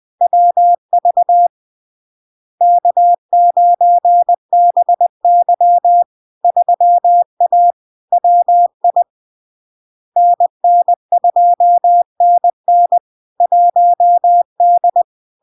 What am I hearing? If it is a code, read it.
Morse code, heard as a keyed tone, about 20 wpm, 700 Hz. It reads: WV K9BY 3A WI NN2NN 1D
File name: LARK-FD-2022-Log-in-CW-20wpm-700-Hz